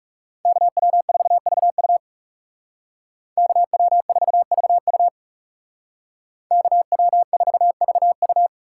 Morse code, heard as a keyed tone, about 39 wpm, 700 Hz. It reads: KW4VU KW4VU KW4VU